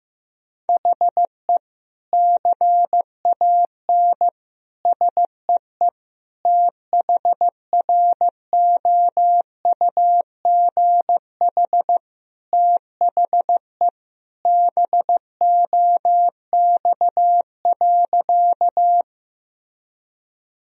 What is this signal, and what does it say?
Morse code, heard as a keyed tone, about 16 wpm, 700 Hz. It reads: HE CAN SEE THROUGH THE BOX.
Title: Sentences from Top 1000 Words - Review Part 2 - 15wpm